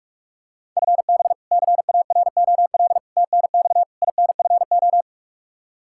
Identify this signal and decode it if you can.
Morse code, heard as a keyed tone, about 41 wpm, 700 Hz. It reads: FB CAROL TNX INFO
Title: Practice QSO - Exchange 3 - More Info - 45wpm